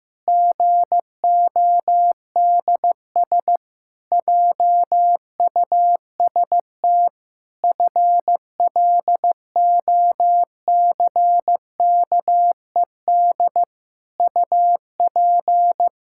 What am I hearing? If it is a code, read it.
Morse code, heard as a keyed tone, about 15 wpm, 700 Hz. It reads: GODS JUST FLOCKED UP
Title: Pangram Challenge v2 - 15wpm